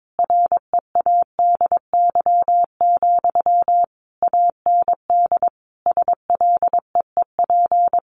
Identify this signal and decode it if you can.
Morse code, heard as a keyed tone, about 22 wpm, 700 Hz. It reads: READY, AND SLEEP
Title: Sentences from Top 500 Words 22wpm